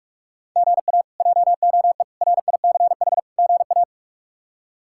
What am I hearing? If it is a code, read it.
Morse code, heard as a keyed tone, about 46 wpm, 700 Hz. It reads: GA JOE RICH GA
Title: K1USN SST QSO Practice - 45wpm